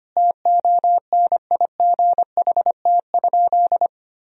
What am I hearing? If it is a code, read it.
Morse code, heard as a keyed tone, about 25 wpm, 700 Hz. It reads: TONIGHT?
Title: Sentences from Top 1000 Words - Review Part 2 - 25wpm